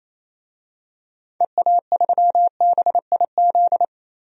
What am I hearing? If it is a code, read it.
Morse code, heard as a keyed tone, about 28 wpm, 700 Hz. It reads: EA3BIZ
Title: DXCC-100 Call Signs - 6 Characters - 28wpm